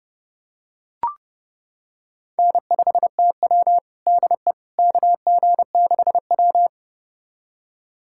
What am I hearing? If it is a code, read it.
Morse code, heard as a keyed tone, about 30 wpm, 700 Hz. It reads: N5TW DE KG6W
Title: USA Callsign DE Callsign - 4 Characters - 30wpm